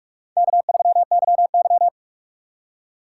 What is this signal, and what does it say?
Morse code, heard as a keyed tone, about 46 wpm, 700 Hz. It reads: K3YY
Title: USA Call Signs - 4 Characters - 2x Character Spacing - 45wpm